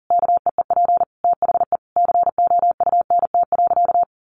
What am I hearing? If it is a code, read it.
Morse code, heard as a keyed tone, about 40 wpm, 700 Hz. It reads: KEEP THE COUNT.
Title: Sentences from Top 1500 Words 40wpm